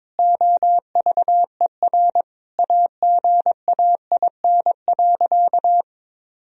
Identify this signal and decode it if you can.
Morse code, heard as a keyed tone, about 22 wpm, 700 Hz. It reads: OVER AGAIN.
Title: Sentences from Top 1000 Words - Review Part 1 - 22wpm